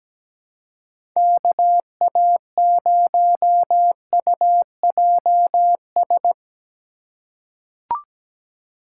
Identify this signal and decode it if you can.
Morse code, heard as a keyed tone, about 17 wpm, 700 Hz. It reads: KA0UJS
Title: USA Call Signs - 6 Characters - 2x Character Spacing - 17wpm